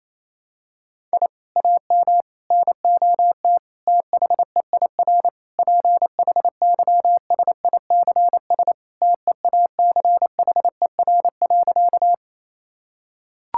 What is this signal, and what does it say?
Morse code, heard as a keyed tone, about 28 wpm, 700 Hz. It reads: I AM NOT THEIR PHYSICS TEACHER.
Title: Sentences from Top 1000 Words - Review Part 6 - 28wpm